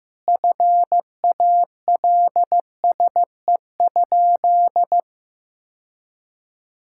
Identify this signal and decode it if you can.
Morse code, heard as a keyed tone, about 15 wpm, 700 Hz. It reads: FALSE?
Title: Sentences from Top 1600 Words 15wpm